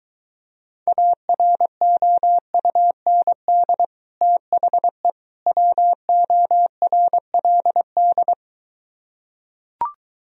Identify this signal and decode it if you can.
Morse code, heard as a keyed tone, about 23 wpm, 700 Hz. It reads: AROUND THE WORLD